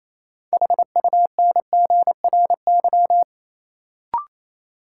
Morse code, heard as a keyed tone, about 28 wpm, 700 Hz. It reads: HUNGRY
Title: Sentences from Top 1400 Words - New Word Drill Alt - 28wpm